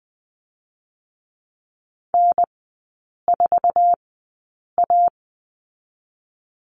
Morse code, heard as a keyed tone, about 20 wpm, 700 Hz. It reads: N 4 A
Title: 047 Lesson 9e - Number 4 - Three Characters 20wpm